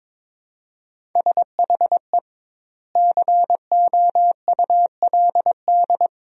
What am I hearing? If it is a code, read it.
Morse code, heard as a keyed tone, about 22 wpm, 700 Hz. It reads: SHE COULD
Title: 2x Word-Spacing - Sentences from Top 500 Words - 22wpm